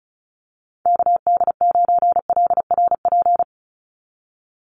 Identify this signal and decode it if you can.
Morse code, heard as a keyed tone, about 34 wpm, 700 Hz. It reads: KD9LRP
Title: USA Call Signs 35wpm